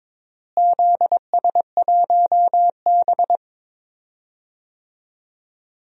Morse code, heard as a keyed tone, about 22 wpm, 700 Hz. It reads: ZS1B
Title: DXCC-100 Call Signs - 4 Characters - Repeated 2x - 22wpm